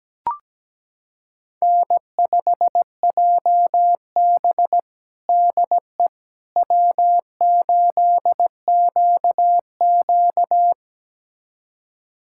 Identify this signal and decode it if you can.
Morse code, heard as a keyed tone, about 17 wpm, 700 Hz. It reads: N5JB DE W8QQ
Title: USA Callsign DE Callsign - 4 Characters - 17wpm